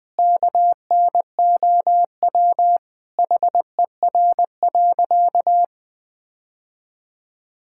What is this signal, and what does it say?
Morse code, heard as a keyed tone, about 20 wpm, 700 Hz. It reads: KNOW HER.
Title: Sentences with 4-Word Phrases - 20wpm